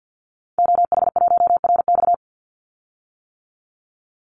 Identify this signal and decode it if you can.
Morse code, heard as a keyed tone, about 50 wpm, 700 Hz. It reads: OH1RX
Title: Call Signs - 50wpm